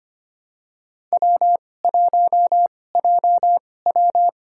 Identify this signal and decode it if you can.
Morse code, heard as a keyed tone, about 25 wpm, spaced as for 18 wpm, 700 Hz. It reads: W1JW
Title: USA Call Signs - 2x Character Spacing - 25wpm